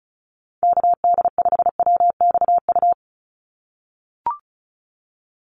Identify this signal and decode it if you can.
Morse code, heard as a keyed tone, about 34 wpm, 700 Hz. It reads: KD5WXU
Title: USA Call Signs 35wpm